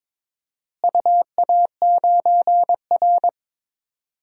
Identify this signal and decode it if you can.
Morse code, heard as a keyed tone, about 22 wpm, 700 Hz. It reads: UA9R
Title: DXCC-100 Call Signs - 4 Characters - Repeated 3x - 22wpm